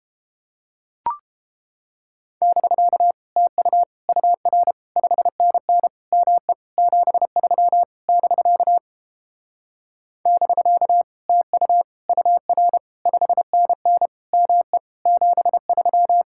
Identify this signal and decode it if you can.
Morse code, heard as a keyed tone, about 29 wpm, 700 Hz. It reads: <BK> TU UR 5NN ME 73 <BK> <BK> TU UR 5NN ME 73